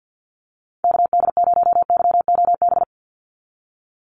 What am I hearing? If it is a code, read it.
Morse code, heard as a keyed tone, about 48 wpm, 700 Hz. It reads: KD0YOB
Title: NA SOTA Activator Callsigns 50wpm